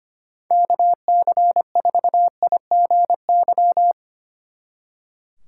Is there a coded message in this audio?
Yes: Morse code from 0.5 to 3.9 s, heard as a keyed tone, about 32 wpm, 700 Hz. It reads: KC4IGY